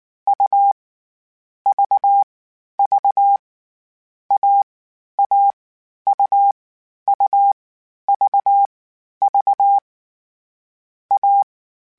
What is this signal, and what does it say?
Morse code, heard as a keyed tone, about 19 wpm, 800 Hz. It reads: U V V A A U U V V A